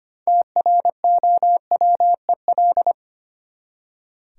Morse code, heard as a keyed tone, about 25 wpm, 690 Hz. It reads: TROWEL
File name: practice / 25wpm / Easy Words with Hints - 25wpm